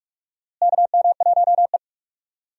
Morse code, heard as a keyed tone, about 46 wpm, 700 Hz. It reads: KM1E
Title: USA Call Signs - 4 Characters - Repeated 2x - 45wpm